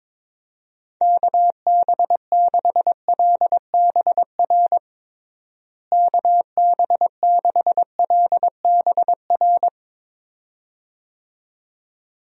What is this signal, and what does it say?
Morse code, heard as a keyed tone, about 21 wpm, 700 Hz. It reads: KB6LBR KB6LBR
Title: USA Call Signs - 6 Characters - Repeated 2x - 22wpm